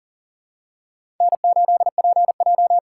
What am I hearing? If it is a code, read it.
Morse code, heard as a keyed tone, about 40 wpm, 700 Hz. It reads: N8PJ
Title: USA Call Signs - 4 Characters - 3x Character Spacing - 40wpm